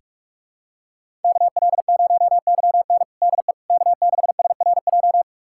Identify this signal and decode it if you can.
Morse code, heard as a keyed tone, about 41 wpm, 700 Hz. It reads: KL0YN DE K6SRJ
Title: USA Callsign DE Callsign - 5 Characters - 45wpm